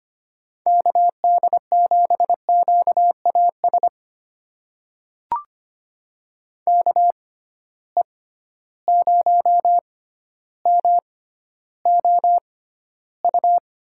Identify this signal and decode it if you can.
Morse code, heard as a keyed tone, about 25 wpm, spaced as for 13 wpm, 700 Hz. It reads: KD7QAS KE0MOU
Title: USA Call Signs - 6 Characters - 6x Character Spacing - 25wpm